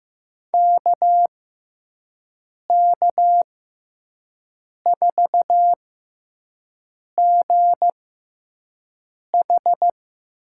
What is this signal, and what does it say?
Morse code, heard as a keyed tone, about 15 wpm, spaced as for 5 wpm, 700 Hz. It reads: KK4GH
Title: USA Call Signs - 6x Character Spacing - 15wpm